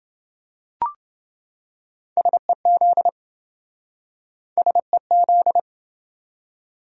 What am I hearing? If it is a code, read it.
Morse code, heard as a keyed tone, about 26 wpm, 700 Hz. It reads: SEZ SEZ